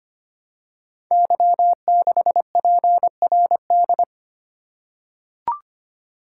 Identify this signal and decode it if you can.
Morse code, heard as a keyed tone, about 25 wpm, 700 Hz. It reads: Y6PRD
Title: Code Groups - 5 Letter:Numbers - 25wpm